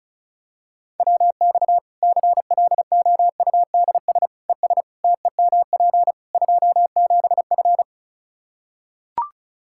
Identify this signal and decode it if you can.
Morse code, heard as a keyed tone, about 36 wpm, 700 Hz. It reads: WX CLOUDS ES TEMP 27F
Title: Practice QSO - Exchange 2 - Info - 35wpm